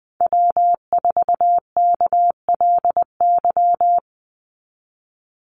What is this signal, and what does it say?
Morse code, heard as a keyed tone, about 19 wpm, 700 Hz. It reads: W4KLY
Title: Call Signs - 20wpm